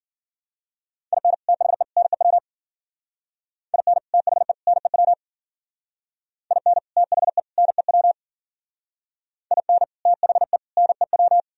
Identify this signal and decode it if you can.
Morse code, heard as a keyed tone, about 50 wpm, 700 Hz. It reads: IN THE NEW IN THE NEW IN THE NEW IN THE NEW